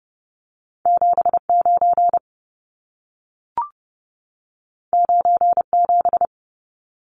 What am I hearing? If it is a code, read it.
Morse code, heard as a keyed tone, about 30 wpm, 700 Hz. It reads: 79 97
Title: Numbers - 2-Digit 30wpm